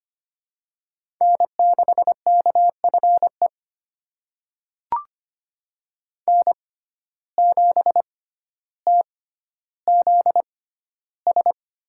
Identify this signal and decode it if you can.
Morse code, heard as a keyed tone, about 25 wpm, spaced as for 13 wpm, 700 Hz. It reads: N6KFE N7TZS
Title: USA Call Signs - 5 Characters - 6x Character Spacing - 25wpm